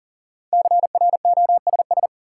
Morse code, heard as a keyed tone, about 40 wpm, 700 Hz. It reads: CROSS
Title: Sentences from Top 300 Words - New Word Drill - 40wpm